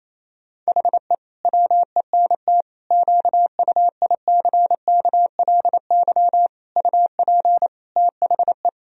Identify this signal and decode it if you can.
Morse code, heard as a keyed tone, about 28 wpm, 700 Hz. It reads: HE WENT QUICKLY UP THE
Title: Sentences from Top 1000 Words - Review Part 2 - 28wpm